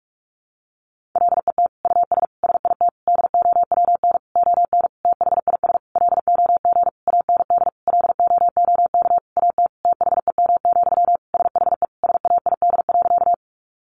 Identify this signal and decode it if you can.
Morse code, heard as a keyed tone, about 44 wpm, 700 Hz. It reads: LET US SIT DOWN ON THIS LOG AND LOOK AT THEM, SHE SAID.
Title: Sentences from Top 1400 Words 45wpm